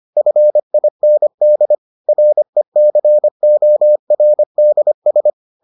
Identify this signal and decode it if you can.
Morse code, heard as a keyed tone, about 25 wpm, 600 Hz. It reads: FIND RECORDS